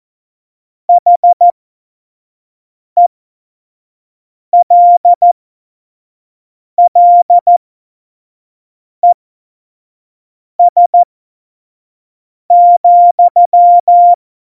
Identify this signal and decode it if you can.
Morse code, heard as a keyed tone, about 14 wpm, spaced as for 5 wpm, 700 Hz. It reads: HELLES,
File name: Morsekurs / Witze / Witz 08. Oktober 2019, 25 F70